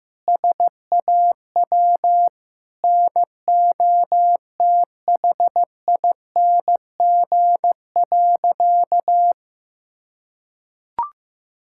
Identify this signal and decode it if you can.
Morse code, heard as a keyed tone, about 15 wpm, 700 Hz. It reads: SAW NOTHING.
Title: Sentences from Top 1000 Words - Review Part 6 - 15wpm